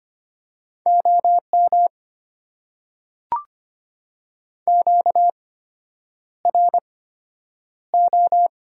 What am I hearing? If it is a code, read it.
Morse code, heard as a keyed tone, about 25 wpm, spaced as for 9 wpm, 700 Hz. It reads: OM QRO
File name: Top 90 QSO Elements - 8x Character Spacing - 25wpm